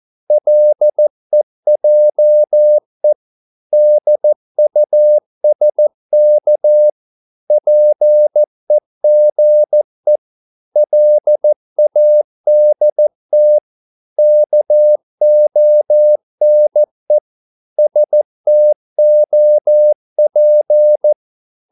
Morse code, heard as a keyed tone, about 14 wpm, 600 Hz. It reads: LEJE DUSK PEGE LADT KONE STOP
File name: Korte ord DK 14wpm